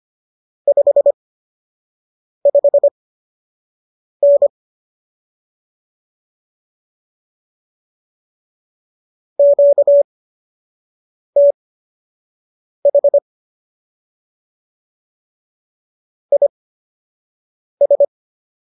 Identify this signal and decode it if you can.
Morse code, heard as a keyed tone, about 25 wpm, spaced as for 6 wpm, 570 Hz. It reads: 55N QTH IS
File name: qso002 06